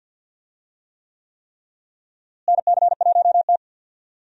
Morse code, heard as a keyed tone, about 52 wpm, 700 Hz. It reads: NC1T
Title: USA Call Signs - 4 Characters - Repeated 3x - 50wpm